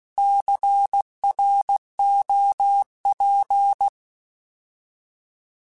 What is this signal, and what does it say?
Morse code, heard as a keyed tone, about 16 wpm, 800 Hz. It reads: CROP
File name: Alphabetical / Alphabetical mp3s / Cs @15